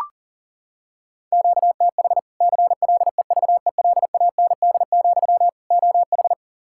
Morse code, heard as a keyed tone, about 40 wpm, 700 Hz. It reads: QTH CLEVELAND, OH
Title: Practice QSO - Exchange 1 - Beginning - 40wpm